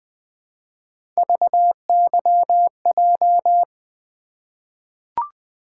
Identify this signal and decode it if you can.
Morse code, heard as a keyed tone, about 20 wpm, 700 Hz. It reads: VYJ
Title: Three Letter-Number Combinations - 20wpm